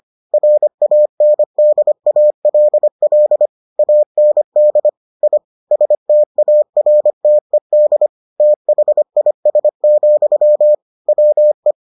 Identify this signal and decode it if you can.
Morse code, heard as a keyed tone, about 25 wpm, 600 Hz. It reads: RANDALL AND I STARTED THIS, WE